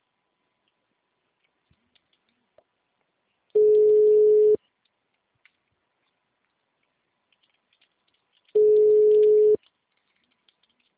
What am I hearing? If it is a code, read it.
Ringback tone (Europe, 425 Hz).